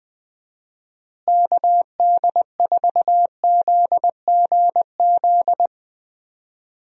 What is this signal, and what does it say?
Morse code, heard as a keyed tone, about 20 wpm, 700 Hz. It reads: KD4ZGZ
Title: USA Call Signs - 6 Characters - Repeated 3x - 20wpm